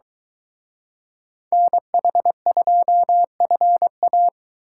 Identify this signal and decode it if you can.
Morse code, heard as a keyed tone, about 23 wpm, 700 Hz. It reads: NH2FA